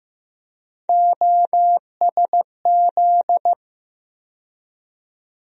Morse code, heard as a keyed tone, about 15 wpm, 700 Hz. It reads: OSZ
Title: USA Call Sign Postfixes - 3 Letters 15wpm